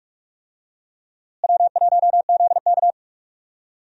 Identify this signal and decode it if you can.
Morse code, heard as a keyed tone, about 46 wpm, 700 Hz. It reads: W1ZK
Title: USA Call Signs - 4 Characters - Repeated 3x - 45wpm